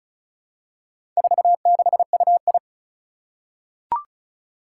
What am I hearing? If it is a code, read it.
Morse code, heard as a keyed tone, about 36 wpm, 700 Hz. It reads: 46UI
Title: Four Challenging Letter-Number Combinations 35wpm